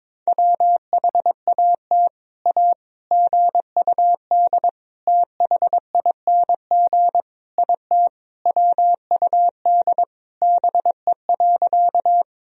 Morse code, heard as a keyed tone, about 22 wpm, 700 Hz. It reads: WHAT A GUD THING IT WUD BE.
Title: Sentences from Top 200 Words and Abbreviations 22wpm